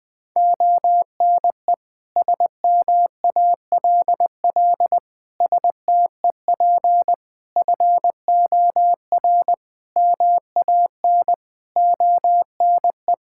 Morse code, heard as a keyed tone, about 20 wpm, 700 Hz. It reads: ONE SMALL STEP FOR MAN ONE
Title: Sets of 9 Words - 20wpm